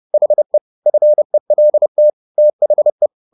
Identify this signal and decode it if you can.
Morse code, heard as a keyed tone, about 30 wpm, 600 Hz. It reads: HE FELT THE